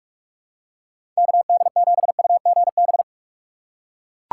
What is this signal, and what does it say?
Morse code, heard as a keyed tone, about 41 wpm, 700 Hz. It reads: KD7UGB
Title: USA Call Signs - 6 Characters - 2x Character Spacing - 45wpm